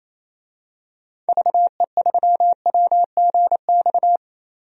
Morse code, heard as a keyed tone, about 28 wpm, 700 Hz. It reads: VE3WGX
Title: DXCC-100 Call Signs - 6 Characters - 28wpm